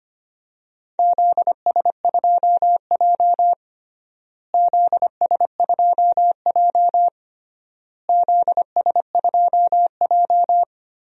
Morse code, heard as a keyed tone, about 25 wpm, 700 Hz. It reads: ZS2J ZS2J ZS2J
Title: DXCC-100 Call Signs - 4 Characters - Repeated 3x - 25wpm